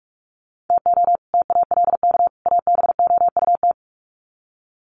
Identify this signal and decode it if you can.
Morse code, heard as a keyed tone, about 44 wpm, 700 Hz. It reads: TO TALK ABOUT
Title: Sets of 3 Words 45wpm